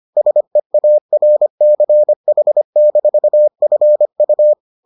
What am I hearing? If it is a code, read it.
Morse code, heard as a keyed tone, about 25 wpm, 600 Hz. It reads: SEARCH-FU